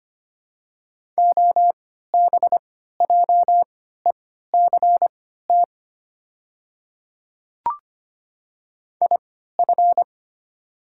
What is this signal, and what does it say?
Morse code, heard as a keyed tone, about 25 wpm, spaced as for 14 wpm, 700 Hz. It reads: OBJECT IF
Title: 3x Character Spacing - Top 500 Words - 25wpm